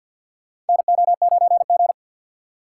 Morse code, heard as a keyed tone, about 53 wpm, 700 Hz. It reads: NO9G
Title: USA Call Signs - 4 Characters - 3x Character Spacing - 50wpm